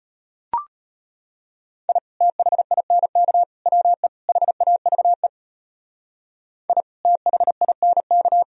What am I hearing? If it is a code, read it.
Morse code, heard as a keyed tone, about 32 wpm, 700 Hz. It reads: I THINK WE HAVE I THINK